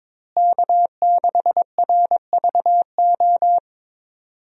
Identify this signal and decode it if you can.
Morse code, heard as a keyed tone, about 22 wpm, 700 Hz. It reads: K6RVO
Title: USA Call Signs - 5 Characters - Repeated 3x - 22wpm